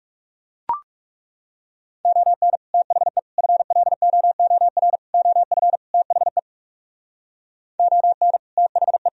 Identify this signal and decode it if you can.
Morse code, heard as a keyed tone, about 39 wpm, 700 Hz. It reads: ON THE FLOOR OF THE ON THE